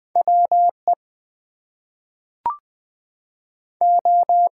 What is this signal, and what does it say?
Morse code, heard as a keyed tone, about 20 wpm, 700 Hz. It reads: WE O
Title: Two Letter Words - 20wpm